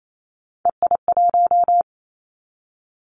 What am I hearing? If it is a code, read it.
Morse code, heard as a keyed tone, about 28 wpm, 700 Hz. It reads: EI1
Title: Callsign Prefixes for Top 50 DXCC - 28wpm